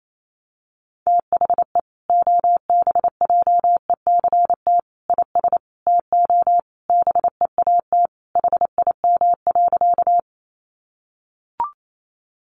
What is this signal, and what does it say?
Morse code, heard as a keyed tone, about 28 wpm, 700 Hz. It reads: THE OBJECT IS TO BEAT HIM.
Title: Sentences from Top 1300 Words 28wpm